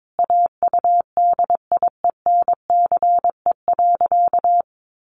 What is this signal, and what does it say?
Morse code, heard as a keyed tone, about 22 wpm, 700 Hz. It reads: AUDIENCE.
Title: Sentences from Top 800 Words 22wpm